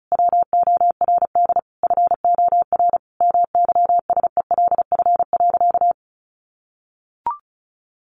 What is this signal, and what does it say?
Morse code, heard as a keyed tone, about 34 wpm, 700 Hz. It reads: WORD FOR MYSELF.
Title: Sentences from Top 1300 Words 35wpm